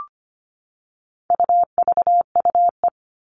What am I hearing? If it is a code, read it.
Morse code, heard as a keyed tone, about 25 wpm, 700 Hz. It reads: UVUE
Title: All Cut Numbers - 4-Digit 25wpm